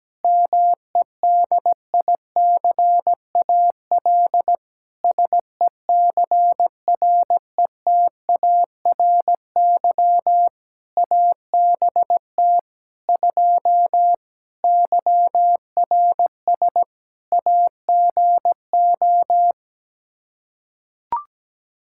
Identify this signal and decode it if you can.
Morse code, heard as a keyed tone, about 17 wpm, 700 Hz. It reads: MEDICAL SECRETARY ABT 2 YRS AGO
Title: Practice QSO - Exchange 3 - More Info - 17wpm